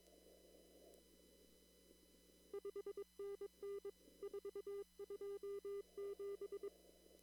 Morse code, heard as a keyed tone, about 22 wpm, 400 Hz. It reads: 5NN 427
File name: CW / kt5j wpx cw-op-k5pi